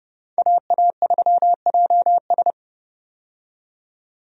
Morse code, heard as a keyed tone, about 29 wpm, 700 Hz. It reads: AA3JS
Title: USA Call Signs - 5 Characters - 30wpm